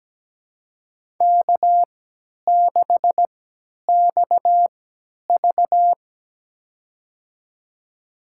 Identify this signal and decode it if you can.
Morse code, heard as a keyed tone, about 17 wpm, spaced as for 10 wpm, 700 Hz. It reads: K6XV
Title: USA Call Signs - 4 Characters - 3x Character Spacing - 17wpm